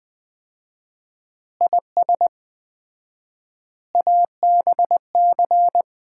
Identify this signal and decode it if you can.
Morse code, heard as a keyed tone, about 20 wpm, 700 Hz. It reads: IS ABC
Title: 4x Word-Spacing - Sets of 5 Words - 20wpm